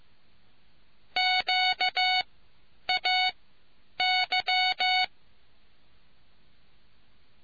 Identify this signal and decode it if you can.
Morse code, heard as a keyed tone, about 15 wpm, spaced as for 9 wpm, 740 Hz. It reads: QAY